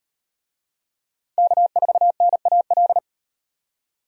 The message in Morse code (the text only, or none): K4NAL